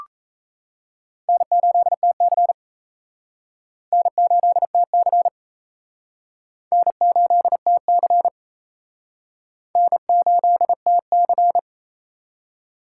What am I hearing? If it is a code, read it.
Morse code, heard as a keyed tone, about 37 wpm, 700 Hz. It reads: N8TC N8TC N8TC N8TC